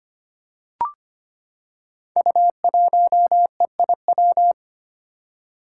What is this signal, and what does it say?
Morse code, heard as a keyed tone, about 25 wpm, 700 Hz. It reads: U1EIW
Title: Five Challenging Letter-Number Combinations 25wpm